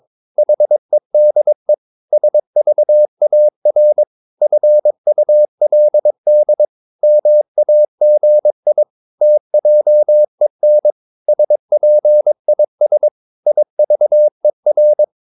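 Morse code, heard as a keyed tone, about 22 wpm, 600 Hz. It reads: HEDE SVAR FULD MAGI TJEN SPIS IVER
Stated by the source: Morse practice of short Danish words at 22 wpm, from CW med Gnister.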